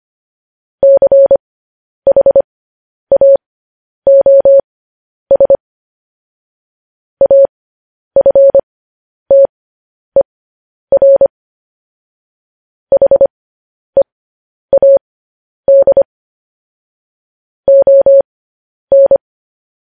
Morse code, heard as a keyed tone, about 25 wpm, spaced as for 10 wpm, 550 Hz. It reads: CHAOS AFTER HEAD ON
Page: News Headlines in Morse Code at 10 WPM